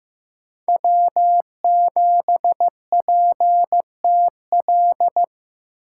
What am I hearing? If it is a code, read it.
Morse code, heard as a keyed tone, about 15 wpm, 700 Hz. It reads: W7PTL
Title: USA Call Signs - 8x Character Spacing - 15wpm